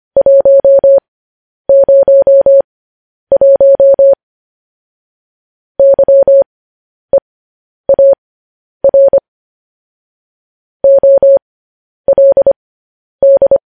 Morse code, heard as a keyed tone, about 25 wpm, spaced as for 10 wpm, 550 Hz. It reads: 101 YEAR OLD
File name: podcast / News headlines-1x-25x10